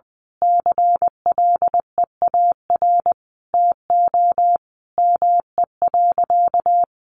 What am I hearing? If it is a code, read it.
Morse code, heard as a keyed tone, about 20 wpm, 700 Hz. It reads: CLEAR TO ME.
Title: Sentences from Top 1300 Words 20wpm